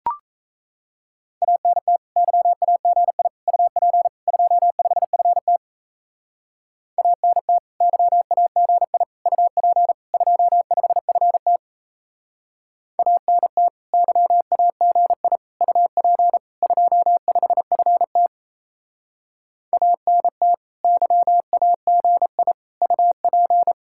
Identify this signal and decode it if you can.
Morse code, heard as a keyed tone, about 37 wpm, 700 Hz. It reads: ANT YAGI UP 25FT ANT YAGI UP 25FT ANT YAGI UP 25FT ANT YAGI UP